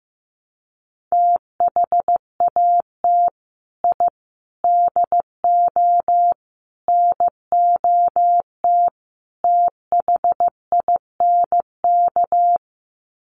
Morse code, heard as a keyed tone, about 15 wpm, 700 Hz. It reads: THAT I DO NOT THINK
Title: Sets of 5 Words Encore 15wpm